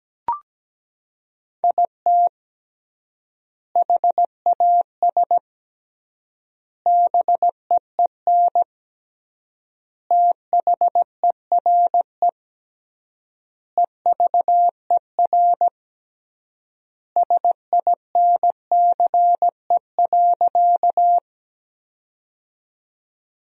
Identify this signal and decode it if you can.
Morse code, heard as a keyed tone, about 17 wpm, 700 Hz. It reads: IT HAS BEEN THERE EVER SINCE.
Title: 3x Word-Spacing - Sentences from Top 400 Words - 17wpm